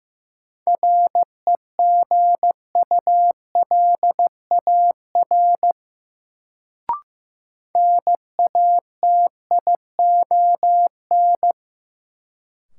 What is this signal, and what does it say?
Morse code, heard as a keyed tone, about 15 wpm, 700 Hz. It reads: REGULAR NATION
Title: Sentences from Top 900 Words - New Word Drill - 15wpm